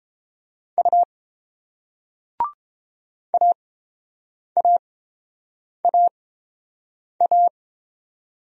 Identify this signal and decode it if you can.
Morse code, heard as a keyed tone, about 31 wpm, spaced as for 7 wpm, 700 Hz. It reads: U AAAA